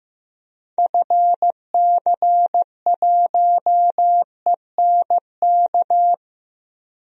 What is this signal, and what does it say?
Morse code, heard as a keyed tone, about 15 wpm, 700 Hz. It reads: FC1ENK
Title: DXCC-100 Call Signs - 6 Characters - 15wpm